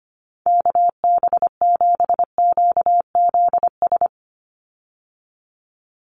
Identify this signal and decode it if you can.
Morse code, heard as a keyed tone, about 25 wpm, 700 Hz. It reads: KB7QZS
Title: USA Call Signs 25wpm